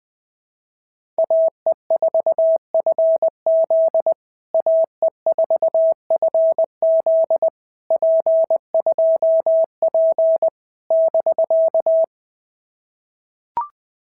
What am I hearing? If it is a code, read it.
Morse code, heard as a keyed tone, about 20 wpm, 650 Hz. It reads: AE4FZ AE4FZ P2P <BK>
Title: POTA QSO Practice - 20wpm